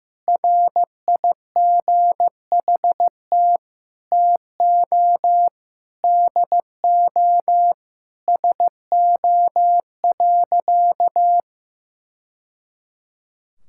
Morse code, heard as a keyed tone, about 15 wpm, 700 Hz. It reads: RIGHT TO DO SO.
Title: Sentences from Top 1000 Words - Review Part 6 - 15wpm